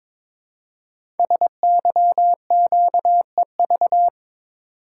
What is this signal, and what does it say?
Morse code, heard as a keyed tone, about 22 wpm, 700 Hz. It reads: SYQEV
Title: Code Groups - 5 Letters - 22wpm